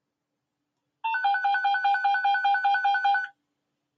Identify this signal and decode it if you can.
There is no code.